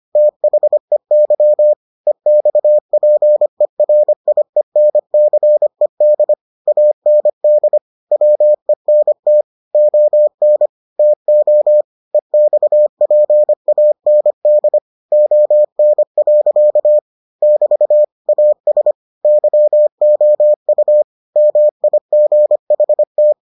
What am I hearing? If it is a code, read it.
Morse code, heard as a keyed tone, about 25 wpm, 600 Hz. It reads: THEY EXPERIENCED AND WENT ON TO EXPAND ON. <BT> AS YOU MIGHT